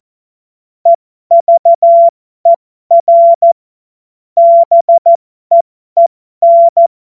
Morse code, heard as a keyed tone, about 14 wpm, spaced as for 12 wpm, 680 Hz. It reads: EVER BEEN